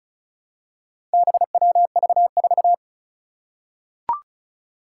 Morse code, heard as a keyed tone, about 36 wpm, 700 Hz. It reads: BWV4
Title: Four Challenging Letter-Number Combinations 35wpm